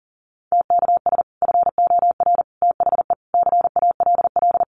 Morse code, heard as a keyed tone, about 40 wpm, 700 Hz. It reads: TKS FOR THE CALL
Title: Common QSO Phrases 40wpm